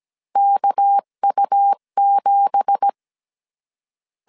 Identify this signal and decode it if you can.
Morse code, heard as a keyed tone, about 17 wpm, 800 Hz. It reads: KU7
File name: 1500C S B 16wpm